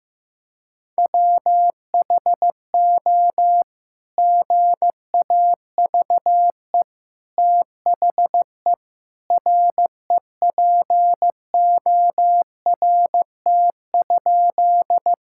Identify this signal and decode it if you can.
Morse code, heard as a keyed tone, about 15 wpm, 700 Hz. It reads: WHO GAVE THE REPORT?
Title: Sentences from Top 1000 Words - Review Part 6 - 15wpm